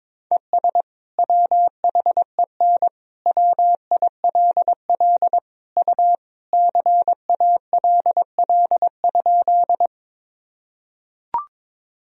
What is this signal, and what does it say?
Morse code, heard as a keyed tone, about 22 wpm, 700 Hz. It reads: ES WHEN WILL U CALL?
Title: Sentences from Top 100 Words and Abbreviations 22wpm